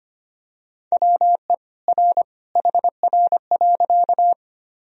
Morse code, heard as a keyed tone, about 25 wpm, 700 Hz. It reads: WE R HR.
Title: Sentences from Top 200 Words and Abbreviations 25wpm